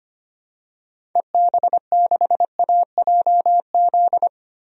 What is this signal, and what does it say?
Morse code, heard as a keyed tone, about 25 wpm, 700 Hz. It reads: EB6AJZ
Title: DXCC-100 Call Signs - 6 Characters - Repeated 3x - 25wpm